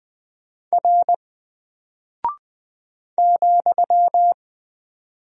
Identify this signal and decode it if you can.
Morse code, heard as a keyed tone, about 20 wpm, 700 Hz. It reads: R ,
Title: Single Character (Letter:Number:Punctuation:Prosign) 20wpm